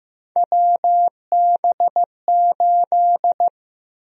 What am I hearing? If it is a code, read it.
Morse code, heard as a keyed tone, about 15 wpm, 700 Hz. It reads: WB8
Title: USA Call Sign Prefixes - Repeated 2x - 15wpm